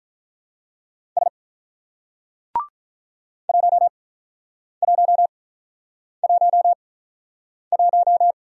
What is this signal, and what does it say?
Morse code, heard as a keyed tone, about 45 wpm, spaced as for 9 wpm, 700 Hz. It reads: S 1111